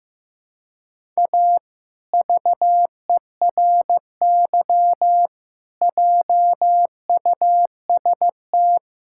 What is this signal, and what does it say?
Morse code, heard as a keyed tone, about 15 wpm, 700 Hz. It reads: A VERY JUST
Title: Sentences from Top 1600 Words 15wpm